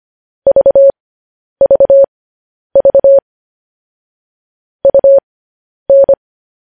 Morse code, heard as a keyed tone, about 25 wpm, spaced as for 10 wpm, 550 Hz. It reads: VVV UN
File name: podcast / News headlines-1x-25x10